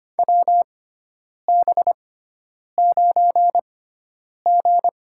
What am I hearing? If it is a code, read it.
Morse code, heard as a keyed tone, about 25 wpm, spaced as for 9 wpm, 700 Hz. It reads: WB9G
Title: USA Call Signs - 6x Character Spacing - 25wpm